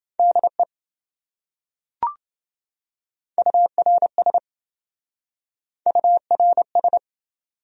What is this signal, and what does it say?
Morse code, heard as a keyed tone, about 26 wpm, 700 Hz. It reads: DE URS URS